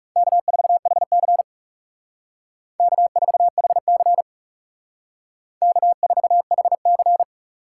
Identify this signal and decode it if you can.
Morse code, heard as a keyed tone, about 38 wpm, 700 Hz. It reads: K4HC K4HC K4HC